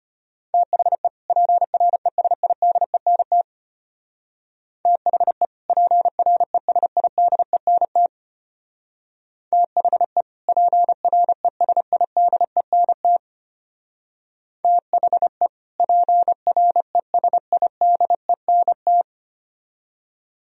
Morse code, heard as a keyed tone, about 31 wpm, 700 Hz. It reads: THE PRESIDENT THE PRESIDENT THE PRESIDENT THE PRESIDENT
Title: Speed-Racing - Sets of 2 Words 25wpm